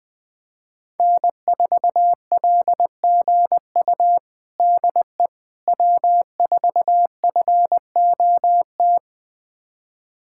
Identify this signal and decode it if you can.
Morse code, heard as a keyed tone, about 20 wpm, 700 Hz. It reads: N4LGU DE W4FOT
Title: USA Callsign DE Callsign - 5 Characters - 20wpm